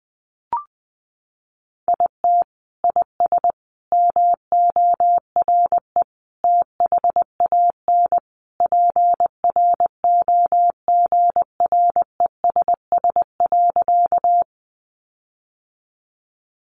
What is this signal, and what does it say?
Morse code, heard as a keyed tone, about 20 wpm, 700 Hz. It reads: IT IS MORE THAN PROGRESS.
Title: Sentences from Top 1400 Words 20wpm